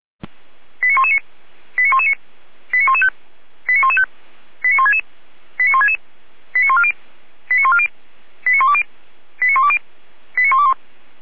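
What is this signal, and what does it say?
Five-tone selective call (ZVEI), heard as quick runs of five tones, tones of about 80 ms. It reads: 89119 89119 89115 89115 89166 89166 89140 89140 89120 89120 89121
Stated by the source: ZVEI five-tone selective call, code 89099, received by SDR and served by feuerwehr-orlamuende (fire brigade Orlamünde).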